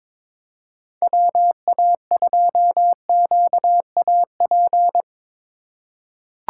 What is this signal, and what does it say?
Morse code, heard as a keyed tone, about 21 wpm, 700 Hz. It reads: WA2QAP
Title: USA Call Signs - 6 Characters - 2x Character Spacing - 22wpm